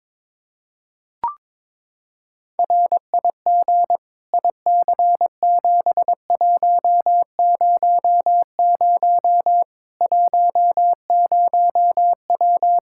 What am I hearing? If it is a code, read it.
Morse code, heard as a keyed tone, about 21 wpm, 700 Hz. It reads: RIG IC7100 10W
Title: Practice QSO - Exchange 2 - Info - 22wpm